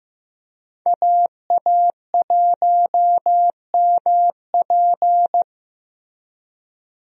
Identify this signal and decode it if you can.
Morse code, heard as a keyed tone, about 15 wpm, 700 Hz. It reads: AA1MP
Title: USA Call Signs - 5 Characters - 15wpm